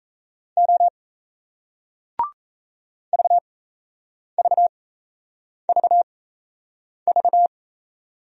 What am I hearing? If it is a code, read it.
Morse code, heard as a keyed tone, about 35 wpm, spaced as for 8 wpm, 700 Hz. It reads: O VVVV
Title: Speed-Racing Single Letters 28wpm